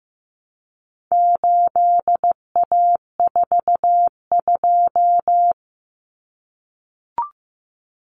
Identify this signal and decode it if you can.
Morse code, heard as a keyed tone, about 15 wpm, 700 Hz. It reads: 8A42
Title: Common Cut Numbers - 4-Digit 15wpm